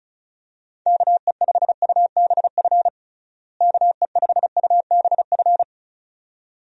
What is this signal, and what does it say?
Morse code, heard as a keyed tone, about 36 wpm, 700 Hz. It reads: KE5UBF KE5UBF
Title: USA Call Signs - 6 Characters - Repeated 2x - 35wpm